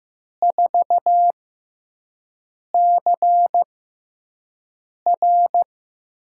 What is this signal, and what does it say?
Morse code, heard as a keyed tone, about 15 wpm, spaced as for 5 wpm, 700 Hz. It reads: 4CR
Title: USA Call Signs - 4 Characters - 6x Character Spacing - 15wpm